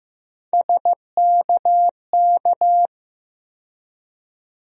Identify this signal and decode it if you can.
Morse code, heard as a keyed tone, about 15 wpm, 700 Hz. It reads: SKK
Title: USA Call Sign Postfixes - 3 Letters 15wpm